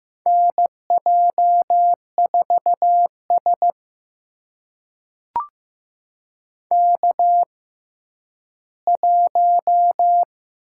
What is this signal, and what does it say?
Morse code, heard as a keyed tone, about 15 wpm, spaced as for 10 wpm, 700 Hz. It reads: NJ4S K1
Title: USA Call Signs - 4 Characters - 6x Character Spacing - 15wpm